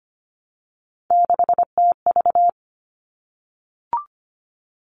Morse code, heard as a keyed tone, about 25 wpm, 700 Hz. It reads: 6TV
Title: All Cut Numbers - 3-Digit 25wpm